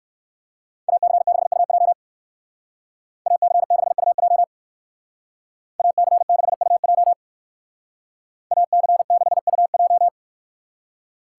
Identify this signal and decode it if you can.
Morse code, heard as a keyed tone, about 60 wpm, 700 Hz. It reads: AC6UJ AC6UJ AC6UJ AC6UJ